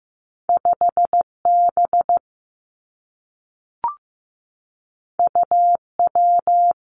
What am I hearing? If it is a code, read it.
Morse code, heard as a keyed tone, about 15 wpm, 700 Hz. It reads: 5 B U W
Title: Two Challenging Letter-Number Combinations 15wpm